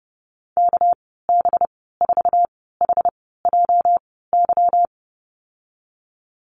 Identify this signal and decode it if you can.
Morse code, heard as a keyed tone, about 32 wpm, spaced as for 17 wpm, 700 Hz. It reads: KB4HJY
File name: Lesson 37g - Letter x - Call Signs 30wpm